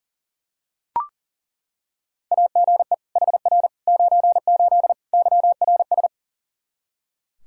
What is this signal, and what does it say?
Morse code, heard as a keyed tone, about 40 wpm, 700 Hz. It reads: AGE HR 98 YRS
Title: K1USN SST QSO Practice 40wpm